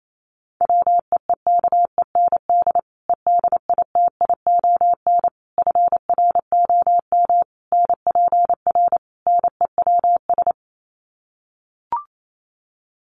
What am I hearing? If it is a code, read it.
Morse code, heard as a keyed tone, about 28 wpm, 700 Hz. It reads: WEEKEND EDITION FROM NPR NEWS
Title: Sets of 5 Words 28wpm